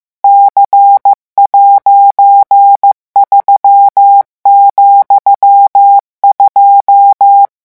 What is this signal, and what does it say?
Morse code, heard as a keyed tone, about 15 wpm, 800 Hz. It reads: C'3,2
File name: morse mp3 / revision13-vitesse 14 mots